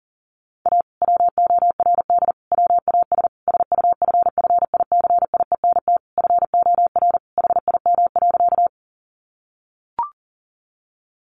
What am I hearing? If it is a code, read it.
Morse code, heard as a keyed tone, about 40 wpm, 700 Hz. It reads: A WORD WAS SUFFICIENT FOR HIM.
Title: Sentences from Top 1400 Words 40wpm